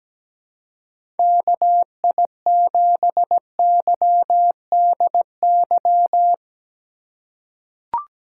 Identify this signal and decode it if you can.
Morse code, heard as a keyed tone, about 17 wpm, 700 Hz. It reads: KI7YDY
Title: USA Call Signs - 6 Characters - 2x Character Spacing - 17wpm